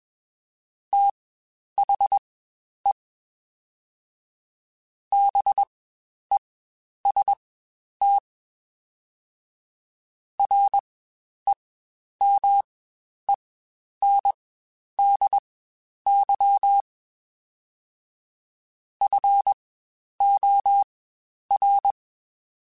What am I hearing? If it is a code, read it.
Morse code, heard as a keyed tone, about 21 wpm, spaced as for 10 wpm, 800 Hz. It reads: THE BEST REMENDY FOR
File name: Quotes Spaced / Quotes Spaced-mp3s / Temper sp @20